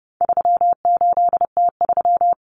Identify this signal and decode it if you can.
Morse code, heard as a keyed tone, about 30 wpm, 700 Hz. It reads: 38T3
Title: Common Cut Numbers - 4-Digit 30wpm